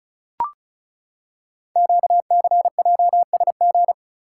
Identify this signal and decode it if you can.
Morse code, heard as a keyed tone, about 36 wpm, 690 Hz. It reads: QCJSG
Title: Code Groups - 5 Letters - 35wpm